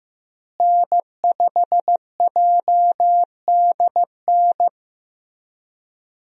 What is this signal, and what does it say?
Morse code, heard as a keyed tone, about 15 wpm, 700 Hz. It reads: N5JDN
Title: USA Call Signs - 5 Characters - Repeated 3x - 15wpm